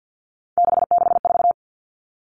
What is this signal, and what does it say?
Morse code, heard as a keyed tone, about 50 wpm, 700 Hz. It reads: 664
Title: Common Cut Numbers - 3-Digit 50wpm